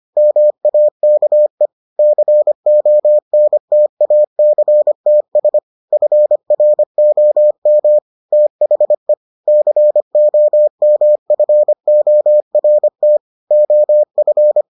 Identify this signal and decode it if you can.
Morse code, heard as a keyed tone, about 25 wpm, 600 Hz. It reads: MAKE CONTACTS FROM THE COMFORT OF